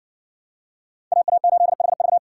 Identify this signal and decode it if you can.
Morse code, heard as a keyed tone, about 58 wpm, 700 Hz. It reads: AA8HV